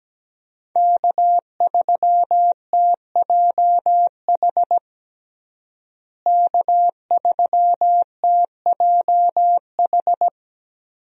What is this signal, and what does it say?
Morse code, heard as a keyed tone, about 17 wpm, 700 Hz. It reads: K3TJH K3TJH
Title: USA Call Signs - 5 Characters - Repeated 3x - 17wpm